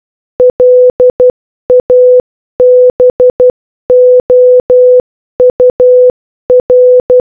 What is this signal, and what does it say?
Morse code, heard as a keyed tone, about 12 wpm, 500 Hz. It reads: LABOUR